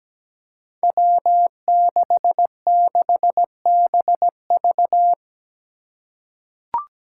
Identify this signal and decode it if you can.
Morse code, heard as a keyed tone, about 17 wpm, 700 Hz. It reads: W66BV
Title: Five Challenging Letter-Number Combinations 17wpm